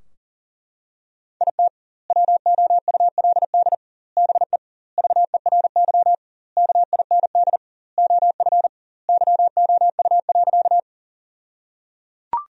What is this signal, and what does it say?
Morse code, heard as a keyed tone, about 37 wpm, 700 Hz. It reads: IT WOULD BE VERY KIND OF YOU.
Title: 2x Word-Spacing - Sentences from Top 200 Words - 40wpm